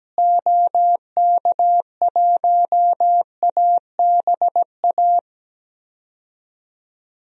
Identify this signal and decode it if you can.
Morse code, heard as a keyed tone, about 17 wpm, 700 Hz. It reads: OK1ABA
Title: DXCC-100 Call Signs - 6 Characters - Repeated 3x - 17wpm